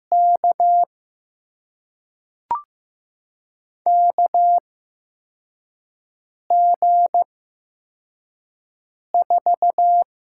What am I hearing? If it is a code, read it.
Morse code, heard as a keyed tone, about 15 wpm, spaced as for 4 wpm, 700 Hz. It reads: K KG4
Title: USA Call Signs - 6 Characters - 8x Character Spacing - 15wpm